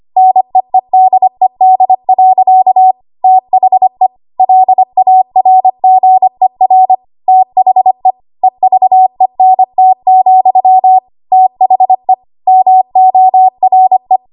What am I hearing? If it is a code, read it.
Morse code, heard as a keyed tone, about 25 wpm, 750 Hz. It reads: NEEDED. THE LARGER THE EVENT, THE MORE
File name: Morse / 25WPM